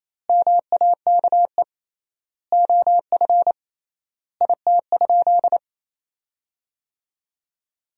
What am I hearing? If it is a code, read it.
Morse code, heard as a keyed tone, about 28 wpm, 700 Hz. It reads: MAKE OF IT?
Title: 3x Word-Spacing - Sentences from Top 100 Words - 28wpm